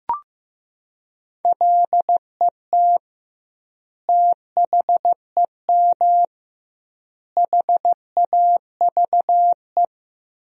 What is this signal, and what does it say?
Morse code, heard as a keyed tone, about 15 wpm, 700 Hz. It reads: LET THEM HAVE
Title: 2x Word-Spacing - Sentences from Top 300 Words - 15wpm